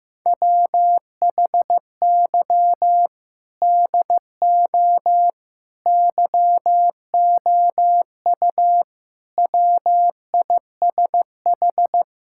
Morse code, heard as a keyed tone, about 16 wpm, 700 Hz. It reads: WHY DO YOU WISH
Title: Sentences from Top 1600 Words 15wpm